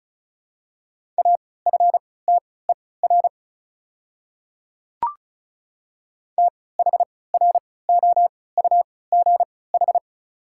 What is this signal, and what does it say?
Morse code, heard as a keyed tone, about 36 wpm, spaced as for 20 wpm, 700 Hz. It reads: AFTER THROUGH
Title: 3x Character Spacing - Top 200 Words - 35wpm